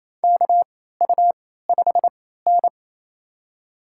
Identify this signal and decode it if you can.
Morse code, heard as a keyed tone, about 28 wpm, spaced as for 16 wpm, 700 Hz. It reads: KU5N
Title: USA Call Signs - 4 Characters - 3x Character Spacing - 28wpm